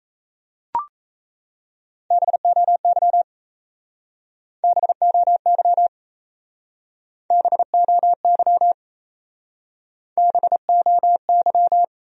Morse code, heard as a keyed tone, about 34 wpm, 700 Hz. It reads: BOY BOY BOY BOY
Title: Speed-Racing Top 300 Words 28wpm